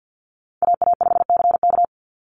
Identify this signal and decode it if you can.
Morse code, heard as a keyed tone, about 50 wpm, 700 Hz. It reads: AA5CK
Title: NA SOTA Chaser Callsigns 50wpm